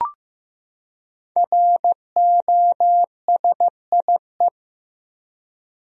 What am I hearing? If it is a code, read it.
Morse code, heard as a keyed tone, about 15 wpm, 700 Hz. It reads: ROSIE
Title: USA Common Names - 15wpm